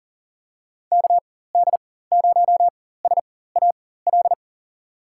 Morse code, heard as a keyed tone, about 39 wpm, spaced as for 18 wpm, 700 Hz. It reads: KD0SAL
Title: USA Call Signs - 6 Characters - 4x Character Spacing - 40wpm